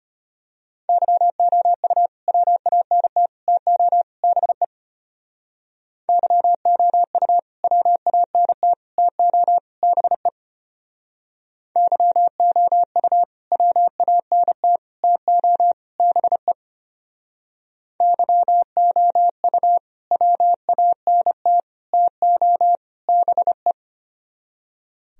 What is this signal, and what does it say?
Morse code, heard as a keyed tone, about 32 wpm, 700 Hz. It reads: YOU WANT TO BE YOU WANT TO BE YOU WANT TO BE YOU WANT TO BE